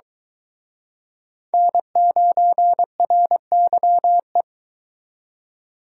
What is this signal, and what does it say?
Morse code, heard as a keyed tone, about 24 wpm, 700 Hz. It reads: N9RYE